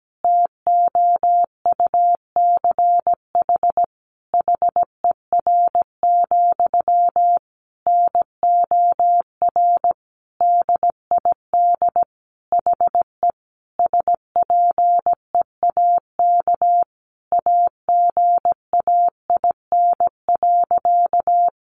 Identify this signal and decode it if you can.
Morse code, heard as a keyed tone, about 17 wpm, 700 Hz. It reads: TOUCH HER, NOR DID HE SPEAK AGAIN.
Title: Sentences from Top 1500 Words 17wpm